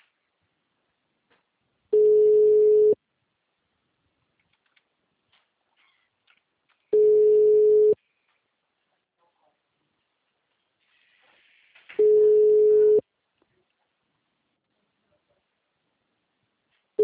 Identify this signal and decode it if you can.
Ringback tone (Europe, 425 Hz).